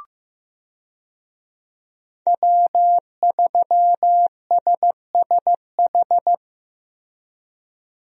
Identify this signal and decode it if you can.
Morse code, heard as a keyed tone, about 15 wpm, 700 Hz. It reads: W3SSH
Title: USA Call Signs - 5 Characters - Repeated 2x - 15wpm